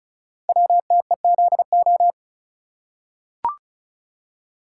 Morse code, heard as a keyed tone, about 36 wpm, 700 Hz. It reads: WTEZO
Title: Code Groups - 5 Letters - 35wpm